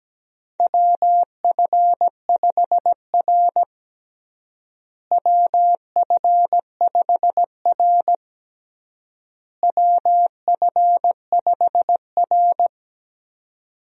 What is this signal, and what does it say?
Morse code, heard as a keyed tone, about 17 wpm, 700 Hz. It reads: WF5R WF5R WF5R
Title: USA Call Signs - 4 Characters - Repeated 3x - 17wpm